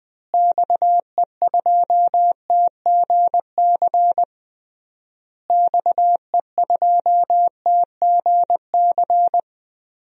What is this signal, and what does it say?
Morse code, heard as a keyed tone, about 20 wpm, 700 Hz. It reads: XE2TGC XE2TGC
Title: DXCC-100 Call Signs - 6 Characters - Repeated 3x - 20wpm